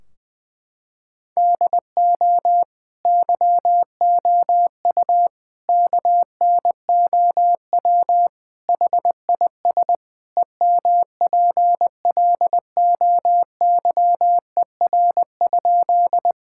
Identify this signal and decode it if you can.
Morse code, heard as a keyed tone, about 20 wpm, 700 Hz. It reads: DO YOU KNOW HIS EMPLOYER?
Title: Sentences from Top 1600 Words 20wpm